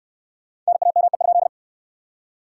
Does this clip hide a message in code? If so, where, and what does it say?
Yes: Morse code from 0.7 to 1.5 s, heard as a keyed tone, about 61 wpm, 700 Hz. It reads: NAME?